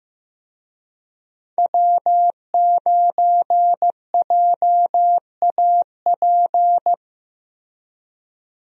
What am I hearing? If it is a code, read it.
Morse code, heard as a keyed tone, about 15 wpm, 700 Hz. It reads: W9JAP
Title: USA Call Signs - 5 Characters - Repeated 3x - 15wpm